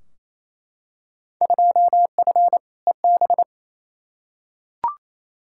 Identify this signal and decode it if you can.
Morse code, heard as a keyed tone, about 28 wpm, 700 Hz. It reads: 2F EB
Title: Field Day Exchanges Encore - 28wpm